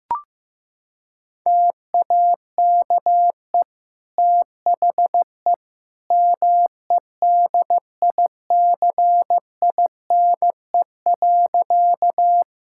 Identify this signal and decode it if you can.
Morse code, heard as a keyed tone, about 16 wpm, 700 Hz. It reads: TAKE THE MEDICINE.
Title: Sentences from Top 1000 Words - Review Part 6 - 15wpm